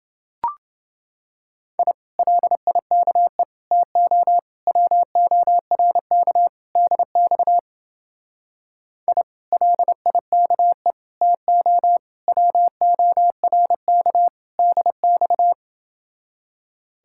Morse code, heard as a keyed tone, about 29 wpm, 700 Hz. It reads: I LIKE TO WORK DX I LIKE TO WORK DX